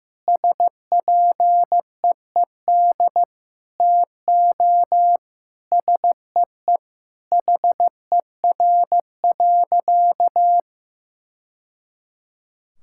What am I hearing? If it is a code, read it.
Morse code, heard as a keyed tone, about 16 wpm, 700 Hz. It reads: SPEED TO SEE HER.
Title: Sentences from Top 1000 Words - Review Part 5 - 15wpm